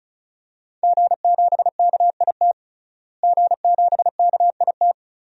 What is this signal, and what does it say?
Morse code, heard as a keyed tone, about 36 wpm, 700 Hz. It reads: G7KIT G7KIT
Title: DXCC-100 Call Signs - 5 Characters - Repeated 2x - 35wpm